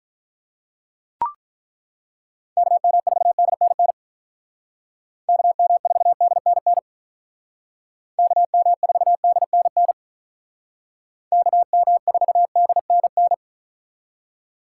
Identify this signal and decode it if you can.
Morse code, heard as a keyed tone, about 43 wpm, 700 Hz. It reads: KM4DNN KM4DNN KM4DNN KM4DNN
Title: Speed Racing - USA Call Signs 35wpm